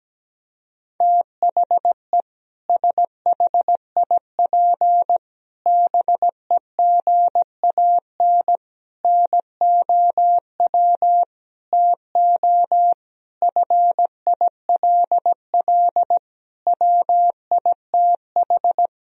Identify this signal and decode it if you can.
Morse code, heard as a keyed tone, about 17 wpm, 700 Hz. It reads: THE SHIP BEGAN NOW TO FILL WITH
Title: Sentences from Top 1000 Words - Review Part 3 - 17wpm